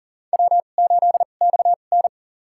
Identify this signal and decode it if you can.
Morse code, heard as a keyed tone, about 40 wpm, spaced as for 29 wpm, 700 Hz. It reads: W8XN
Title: USA Call Signs - 4 Characters - 2x Character Spacing - 40wpm